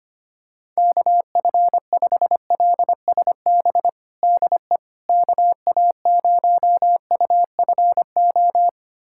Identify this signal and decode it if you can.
Morse code, heard as a keyed tone, about 25 wpm, 700 Hz. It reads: KF5LSB DE KA0UFO
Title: USA Callsign DE Callsign 25wpm